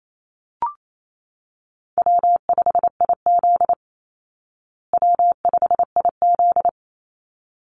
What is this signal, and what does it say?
Morse code, heard as a keyed tone, about 28 wpm, 700 Hz. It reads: W5IZ W5IZ
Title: Call Signs Repeated 3x - 28wpm